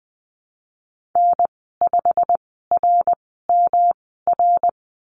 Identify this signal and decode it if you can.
Morse code, heard as a keyed tone, about 20 wpm, spaced as for 14 wpm, 700 Hz. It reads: N5RMR
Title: 173 Lesson 28g - Number 9 - Call Signs 20wpm